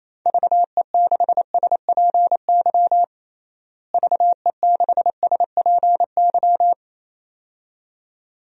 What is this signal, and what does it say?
Morse code, heard as a keyed tone, about 28 wpm, 700 Hz. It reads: VE6SPY VE6SPY
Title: DXCC-100 Call Signs - 6 Characters - Repeated 2x - 28wpm